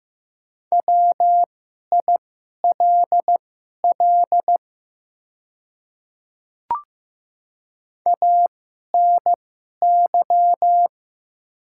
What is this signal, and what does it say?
Morse code, heard as a keyed tone, about 15 wpm, spaced as for 11 wpm, 700 Hz. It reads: WILL ANY
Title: 2x Character Spacing - Top 200 Words - 15wpm